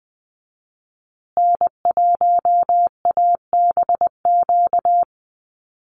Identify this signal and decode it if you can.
Morse code, heard as a keyed tone, about 20 wpm, 700 Hz. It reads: N1ABQ
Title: USA Call Signs Repeated 3x 20wpm